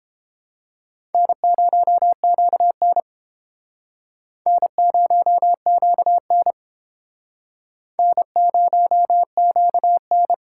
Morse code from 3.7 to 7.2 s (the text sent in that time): N0QN